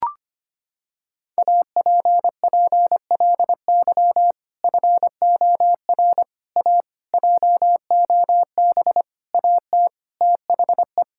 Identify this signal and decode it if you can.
Morse code, heard as a keyed tone, about 25 wpm, 700 Hz. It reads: APPLY FOR A JOB AT THE
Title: Sentences from Top 1600 Words 25wpm